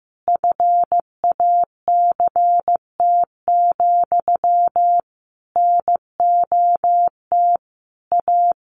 Morse code, heard as a keyed tone, about 15 wpm, 700 Hz. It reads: FACT, NOT A
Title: Sentences from Top 1400 Words 15wpm